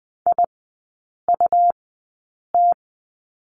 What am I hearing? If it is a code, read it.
Morse code, heard as a keyed tone, about 20 wpm, spaced as for 8 wpm, 700 Hz. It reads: IUT
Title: 100 Lesson 17e - Letter u - Three Characters 20wpm